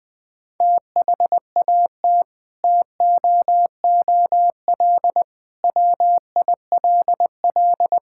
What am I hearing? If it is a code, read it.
Morse code, heard as a keyed tone, about 20 wpm, 700 Hz. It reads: THAT TOOL WILL
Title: Sentences from Top 1000 Words - Review Part 6 - 20wpm